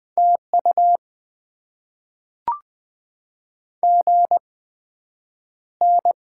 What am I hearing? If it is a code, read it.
Morse code, heard as a keyed tone, about 20 wpm, spaced as for 9 wpm, 700 Hz. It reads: TU GN
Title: Top 90 QSO Elements - 8x Character Spacing - 20wpm